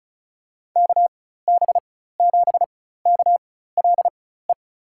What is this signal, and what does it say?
Morse code, heard as a keyed tone, about 36 wpm, spaced as for 16 wpm, 700 Hz. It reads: KB7KLE
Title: USA Call Signs - 4x Character Spacing - 35wpm